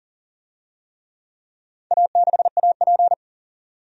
Morse code, heard as a keyed tone, about 40 wpm, 700 Hz. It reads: A6AP
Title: DXCC-100 Call Signs - 4 Characters - 40wpm